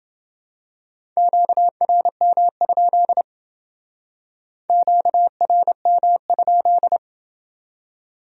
Morse code, heard as a keyed tone, about 28 wpm, 700 Hz. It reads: QRM? QRM?